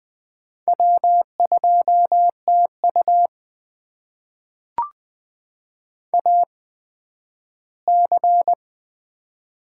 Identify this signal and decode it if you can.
Morse code, heard as a keyed tone, about 20 wpm, spaced as for 12 wpm, 700 Hz. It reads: W2TU AC
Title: USA Call Signs - 4 Characters - 8x Character Spacing - 20wpm